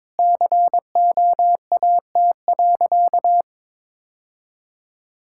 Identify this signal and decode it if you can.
Morse code, heard as a keyed tone, about 21 wpm, 700 Hz. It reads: COAT.
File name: practice / 22wpm / Sentences with 2-Word Phrases - 22wpm